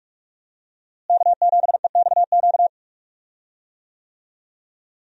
Morse code, heard as a keyed tone, about 46 wpm, 700 Hz. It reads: K7EXQ
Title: USA Call Signs - 5 Characters - Repeated 3x - 45wpm